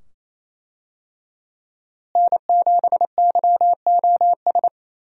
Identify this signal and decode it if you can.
Morse code, heard as a keyed tone, about 28 wpm, 700 Hz. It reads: N7YOS
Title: USA Call Signs - 5 Characters - Repeated 2x - 28wpm